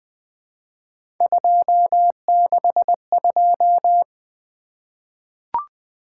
Morse code, heard as a keyed tone, about 20 wpm, 700 Hz. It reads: 262
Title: Three Letter-Number Combinations - 20wpm